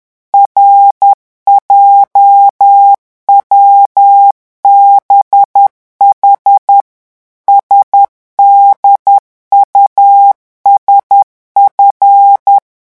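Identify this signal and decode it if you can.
Morse code, heard as a keyed tone, about 10 wpm, 800 Hz. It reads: RJWBH SDUSF
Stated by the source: Morse revision exercise for French radio amateurs at 10 words per minute.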